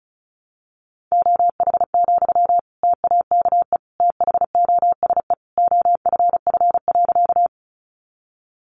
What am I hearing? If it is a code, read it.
Morse code, heard as a keyed tone, about 34 wpm, 700 Hz. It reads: OH, TAKE THOSE OFF.
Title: Sentences from Top 500 Words 35wpm